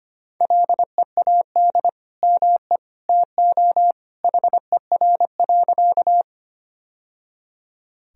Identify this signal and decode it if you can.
Morse code, heard as a keyed tone, about 25 wpm, 700 Hz. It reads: LEAD ME TO HER.
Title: Sentences from Top 1000 Words - Review Part 2 - 25wpm